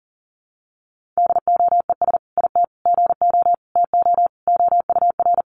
Morse code, heard as a keyed tone, about 40 wpm, 700 Hz. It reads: DOES IT GO TO OUR
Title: Sentences from Top 1500 Words 40wpm